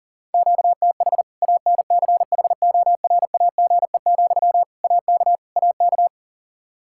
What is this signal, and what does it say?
Morse code, heard as a keyed tone, about 40 wpm, 700 Hz. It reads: QTH ANCHORAGE, AK AK
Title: Practice QSO - Exchange 1 - Beginning - 40wpm